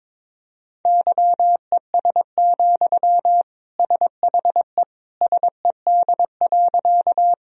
Morse code, heard as a keyed tone, about 22 wpm, 700 Hz. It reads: YES, SHE SED.
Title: Sentences from Top 500 Words and Abbreviations 22wpm